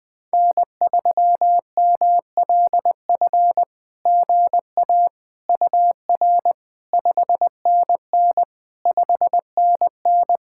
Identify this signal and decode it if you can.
Morse code, heard as a keyed tone, about 20 wpm, 700 Hz. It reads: N3MLF GA UR 5NN 5NN
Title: SOTA QSO Practice 20wpm